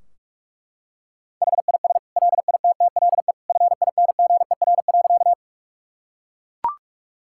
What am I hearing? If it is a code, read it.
Morse code, heard as a keyed tone, about 41 wpm, 700 Hz. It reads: HIS LITTLE FINGER.
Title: Sentences from Top 1600 Words 45wpm